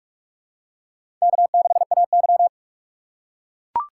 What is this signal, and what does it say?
Morse code, heard as a keyed tone, about 44 wpm, 690 Hz. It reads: K6AY
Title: USA Call Signs - 4 Characters - 2x Character Spacing - 45wpm